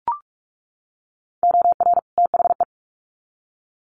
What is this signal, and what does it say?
Morse code, heard as a keyed tone, about 44 wpm, 700 Hz. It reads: OR THE
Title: Sets of 2 Words 45wpm